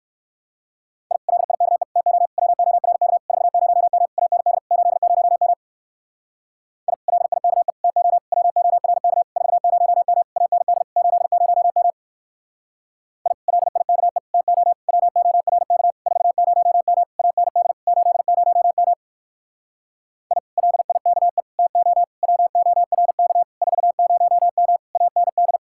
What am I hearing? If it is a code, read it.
Morse code, heard as a keyed tone, about 61 wpm, 700 Hz. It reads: I LIKE TO WORK 40M AND 80M I LIKE TO WORK 40M AND 80M I LIKE TO WORK 40M AND 80M I LIKE TO WORK 40M AND